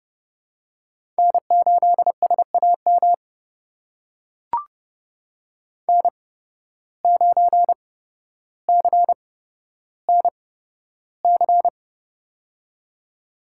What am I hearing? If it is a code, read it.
Morse code, heard as a keyed tone, about 30 wpm, spaced as for 13 wpm, 700 Hz. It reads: N8SAM N9CNC
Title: USA Call Signs - 5 Characters - 8x Character Spacing - 30wpm